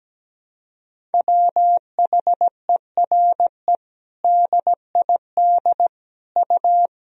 Morse code, heard as a keyed tone, about 17 wpm, 700 Hz. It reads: WHERE DID U
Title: Sentences from Top 200 Words and Abbreviations 17wpm